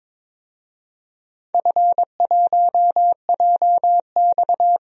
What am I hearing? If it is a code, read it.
Morse code, heard as a keyed tone, about 22 wpm, 700 Hz. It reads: F1JX
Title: DXCC-100 Call Signs - 4 Characters - 22wpm